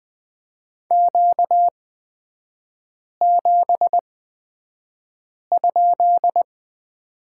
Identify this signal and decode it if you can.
Morse code, heard as a keyed tone, about 20 wpm, spaced as for 5 wpm, 700 Hz. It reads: Q7?
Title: 243 Lesson 39e - Letter z - Three Characters 20wpm